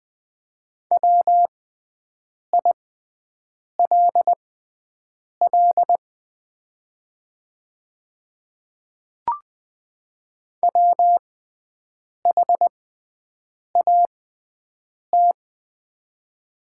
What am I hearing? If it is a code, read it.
Morse code, heard as a keyed tone, about 20 wpm, spaced as for 7 wpm, 700 Hz. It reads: WILL WHAT
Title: 6x Character Spacing - Top 100 Words - 20wpm